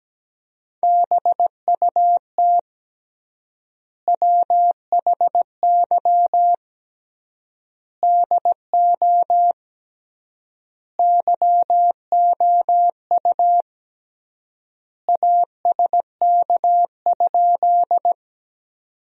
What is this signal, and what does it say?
Morse code, heard as a keyed tone, about 17 wpm, 700 Hz. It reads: BUT WHY DO YOU ASK?
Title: 3x Word-Spacing - Sentences from Top 200 Words - 17wpm